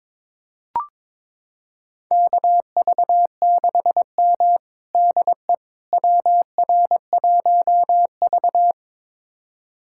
Morse code, heard as a keyed tone, about 22 wpm, 700 Hz. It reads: KV6M DE WR1V
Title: USA Callsign DE Callsign - 4 Characters - 22wpm